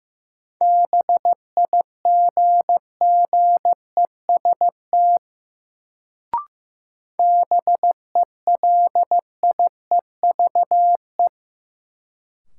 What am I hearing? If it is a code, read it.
Morse code, heard as a keyed tone, about 15 wpm, 700 Hz. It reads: BIGGEST BELIEVE
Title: Sets of 4 Words - New Word Drill - 15wpm